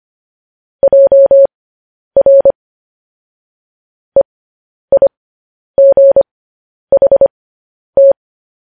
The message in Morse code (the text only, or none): JR EIGHT